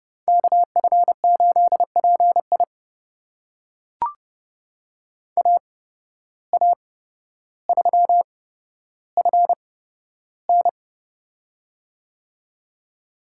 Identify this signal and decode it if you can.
Morse code, heard as a keyed tone, about 30 wpm, spaced as for 13 wpm, 700 Hz. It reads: KF8PI AA3FN
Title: USA Call Signs - 5 Characters - 8x Character Spacing - 30wpm